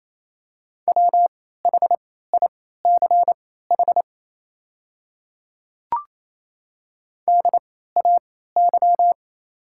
Morse code, heard as a keyed tone, about 28 wpm, spaced as for 16 wpm, 700 Hz. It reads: WHICH DAY
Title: 3x Character Spacing - Top 100 Words - 28wpm